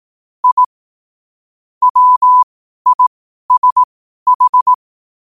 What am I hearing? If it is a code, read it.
Morse code, heard as a keyed tone, about 18 wpm, spaced as for 12 wpm, 1000 Hz. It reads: I WISH
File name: qotdfiles / 12wpm